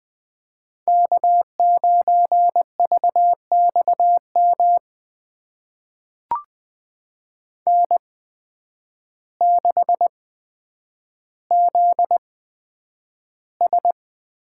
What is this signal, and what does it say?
Morse code, heard as a keyed tone, about 20 wpm, spaced as for 9 wpm, 700 Hz. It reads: K9VXM N6ZS
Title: USA Call Signs - 8x Character Spacing - 20wpm